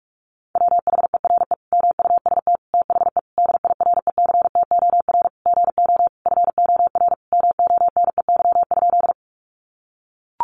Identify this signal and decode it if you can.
Morse code, heard as a keyed tone, about 44 wpm, 700 Hz. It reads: WHERE MUST THE DIRECTOR GO FOR MONEY?
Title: Sentences from Top 1300 Words 45wpm